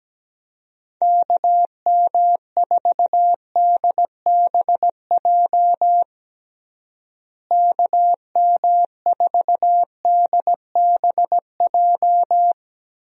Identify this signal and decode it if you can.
Morse code, heard as a keyed tone, about 17 wpm, 700 Hz. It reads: KM4DBJ KM4DBJ
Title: USA Call Signs - 6 Characters - Repeated 2x - 17wpm